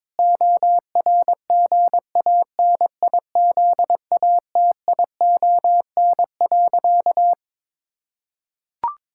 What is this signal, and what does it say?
Morse code, heard as a keyed tone, about 22 wpm, 700 Hz. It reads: ORGANIZATION.
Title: Sentences from Top 1000 Words - Review Part 5 - 22wpm